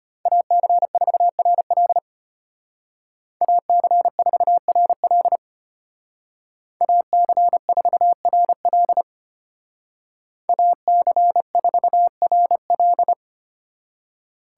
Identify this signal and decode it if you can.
Morse code, heard as a keyed tone, about 33 wpm, 700 Hz. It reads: AC4RL AC4RL AC4RL AC4RL